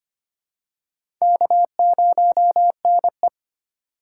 Morse code, heard as a keyed tone, about 25 wpm, 700 Hz. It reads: K0NE
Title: USA Call Signs - 4 Characters - 2x Character Spacing - 25wpm